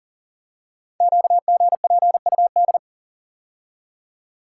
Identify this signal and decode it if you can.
Morse code, heard as a keyed tone, about 39 wpm, 690 Hz. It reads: QGPUD
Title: Code Groups - 5 Letters - 40wpm